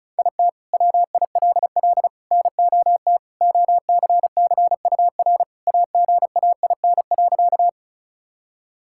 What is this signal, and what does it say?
Morse code, heard as a keyed tone, about 34 wpm, 700 Hz. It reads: IT WILL NOT OCCUR AGAIN.
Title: Sentences from Top 1000 Words - Review Part 6 - 35wpm